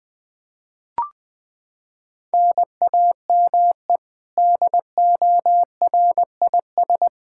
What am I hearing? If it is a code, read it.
Morse code, heard as a keyed tone, about 21 wpm, 700 Hz. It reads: NAME DORIS
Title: Practice QSO - Exchange 1 - Beginning - 20wpm